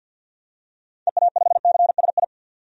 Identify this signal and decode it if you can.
Morse code, heard as a keyed tone, about 51 wpm, 700 Hz. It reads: EA5CSI
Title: DXCC-100 Call Signs - 6 Characters - 50wpm